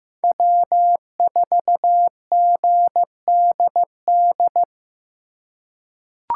Morse code, heard as a keyed tone, about 15 wpm, 700 Hz. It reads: W4GDD
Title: USA Call Signs - 4x Character Spacing - 15wpm